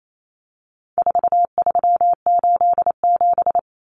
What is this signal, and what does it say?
Morse code, heard as a keyed tone, about 28 wpm, 700 Hz. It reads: 4387
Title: Numbers - 4-Digit 28wpm